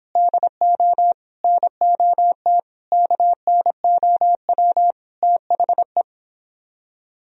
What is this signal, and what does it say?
Morse code, heard as a keyed tone, about 29 wpm, 700 Hz. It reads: DO NOT KNOW THE